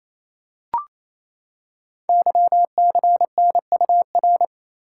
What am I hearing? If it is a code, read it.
Morse code, heard as a keyed tone, about 28 wpm, 700 Hz. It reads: YCNUR
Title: Code Groups - 5 Letter:Numbers - 28wpm